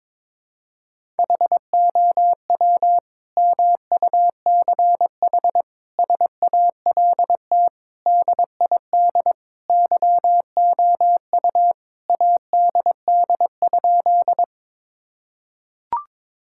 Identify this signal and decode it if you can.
Morse code, heard as a keyed tone, about 22 wpm, 700 Hz. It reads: HOW MUCH SALT DID YOU ADD?
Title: Sentences from Top 1000 Words - Review Part 5 - 22wpm